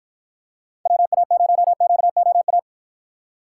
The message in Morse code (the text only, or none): WA0QOA